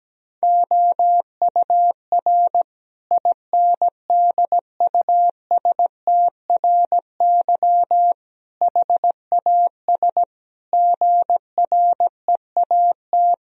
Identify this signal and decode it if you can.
Morse code, heard as a keyed tone, about 17 wpm, 700 Hz. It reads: OUR INDUSTRY HAS GREAT
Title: Sentences from Top 1000 Words - Review Part 4 - 17wpm